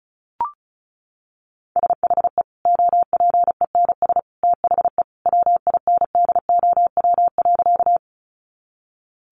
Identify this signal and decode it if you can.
Morse code, heard as a keyed tone, about 34 wpm, 700 Hz. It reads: SHE OPENS THE WINDOW.
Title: Easy Phrases 35wpm